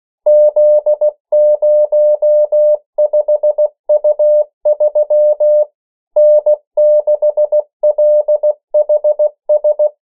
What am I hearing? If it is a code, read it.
Morse code, heard as a keyed tone, about 16 wpm, 600 Hz. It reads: Z05U3 N6LHS
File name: morse mp3 / 14 urok 13